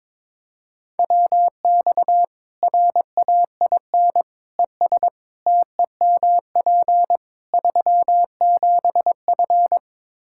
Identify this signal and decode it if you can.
Morse code, heard as a keyed tone, about 22 wpm, 700 Hz. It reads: WX RAIN ES TEMP 37F
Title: Practice QSO - Exchange 2 - Info - 22wpm